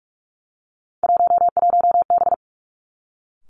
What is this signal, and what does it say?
Morse code, heard as a keyed tone, about 46 wpm, 700 Hz. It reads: 11B
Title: Three Challenging Letter-Number Combinations 45wpm